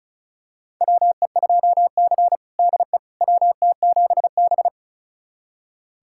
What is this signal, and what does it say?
Morse code, heard as a keyed tone, about 34 wpm, 700 Hz. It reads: WE2C DE WT7B
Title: USA Callsign DE Callsign - 4 Characters - 35wpm